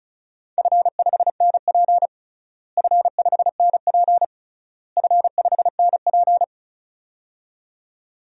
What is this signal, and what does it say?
Morse code, heard as a keyed tone, about 36 wpm, 700 Hz. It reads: F5NP F5NP F5NP
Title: DXCC-100 Call Signs - 4 Characters - Repeated 3x - 35wpm